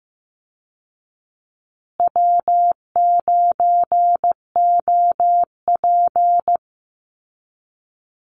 Morse code, heard as a keyed tone, about 15 wpm, 700 Hz. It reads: W9OP
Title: Call Signs Repeated 2x - 15wpm